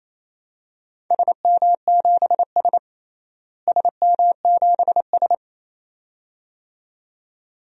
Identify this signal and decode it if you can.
Morse code, heard as a keyed tone, about 28 wpm, 700 Hz. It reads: SM7S SM7S
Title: DXCC-100 Call Signs - 4 Characters - Repeated 2x - 28wpm